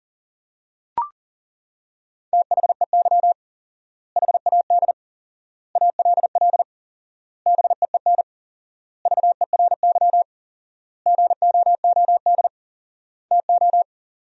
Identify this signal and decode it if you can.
Morse code, heard as a keyed tone, about 40 wpm, 700 Hz. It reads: THEY HAD ALL BEEN VERY GOOD TO
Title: 4x Word-Spacing - Sentences from Top 200 Words - 40wpm